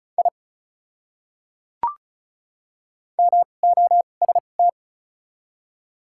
Morse code, heard as a keyed tone, about 34 wpm, spaced as for 25 wpm, 700 Hz. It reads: I MOST
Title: 2x Character Spacing - Top 100 Words - 35wpm